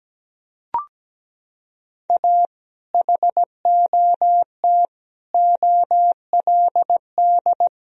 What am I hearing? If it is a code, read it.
Morse code, heard as a keyed tone, about 17 wpm, 700 Hz. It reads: A HOT OLD
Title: Sentences from Top 1000 Words - Review Part 3 - 17wpm